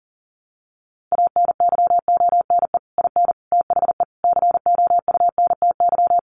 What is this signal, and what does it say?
Morse code, heard as a keyed tone, about 40 wpm, 700 Hz. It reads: ANYONE IN THE COUNTY
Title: Sentences from Top 1200 Words 40wpm